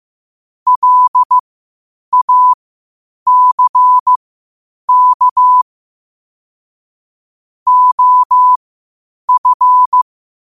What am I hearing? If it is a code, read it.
Morse code, heard as a keyed tone, about 15 wpm, spaced as for 8 wpm, 1000 Hz. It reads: LACK OF
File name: qotdfiles / 8wpm